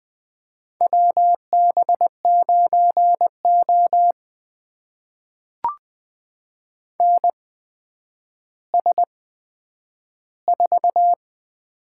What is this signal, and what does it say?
Morse code, heard as a keyed tone, about 20 wpm, spaced as for 10 wpm, 700 Hz. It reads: WB9O NS4
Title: USA Call Signs - 4 Characters - 8x Character Spacing - 20wpm